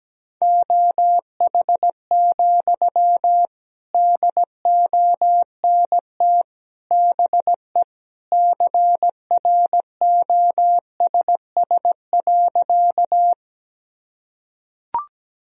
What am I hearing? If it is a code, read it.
Morse code, heard as a keyed tone, about 17 wpm, 700 Hz. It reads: OH, DONT BE CROSS.
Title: Sentences from Top 1000 Words - Review Part 1 - 17wpm